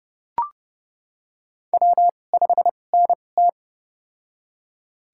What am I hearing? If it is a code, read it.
Morse code, heard as a keyed tone, about 30 wpm, spaced as for 22 wpm, 700 Hz. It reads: W5NT
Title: USA Call Signs - 4 Characters - 2x Character Spacing - 30wpm